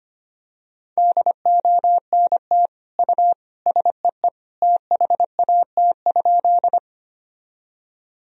Morse code, heard as a keyed tone, about 25 wpm, 700 Hz. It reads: DONT U SEE THAT?
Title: Sentences from Top 300 Words and Abbreviations 25wpm